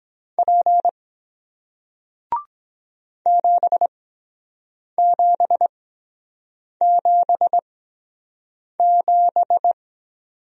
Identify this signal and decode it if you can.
Morse code, heard as a keyed tone, about 23 wpm, spaced as for 7 wpm, 700 Hz. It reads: P 7777